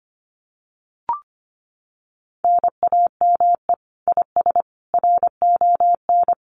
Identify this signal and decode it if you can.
Morse code, heard as a keyed tone, about 25 wpm, 700 Hz. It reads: NAME IS RON
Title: Common QSO Phrases 25wpm